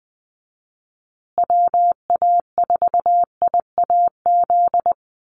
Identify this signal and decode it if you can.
Morse code, heard as a keyed tone, about 20 wpm, 700 Hz. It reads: WA4IAZ
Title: USA Call Signs Repeated 3x 20wpm